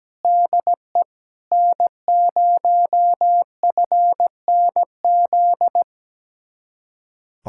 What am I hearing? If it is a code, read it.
Morse code, heard as a keyed tone, about 17 wpm, 700 Hz. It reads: DE N0FNZ
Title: Speed-Racing Common QSO Phrases Encore - 17wpm